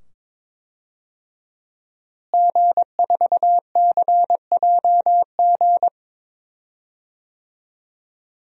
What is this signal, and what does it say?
Morse code, heard as a keyed tone, about 22 wpm, 700 Hz. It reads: G4CJG
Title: DXCC-100 Call Signs - 5 Characters - Repeated 2x - 22wpm